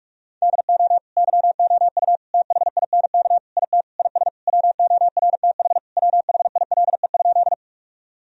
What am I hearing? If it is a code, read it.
Morse code, heard as a keyed tone, about 46 wpm, 700 Hz. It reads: DO YOU THINK IT IS WORTH WHILE?
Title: Sentences from Top 1000 Words - Review Part 1 - 45wpm